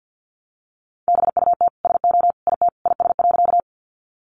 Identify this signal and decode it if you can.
Morse code, heard as a keyed tone, about 50 wpm, 700 Hz. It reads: BUT SO IT IS.
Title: Sentences from Top 100 Words 50wpm